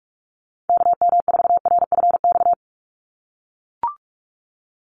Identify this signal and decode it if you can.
Morse code, heard as a keyed tone, about 44 wpm, 700 Hz. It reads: KM4RFX
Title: USA Call Signs 45wpm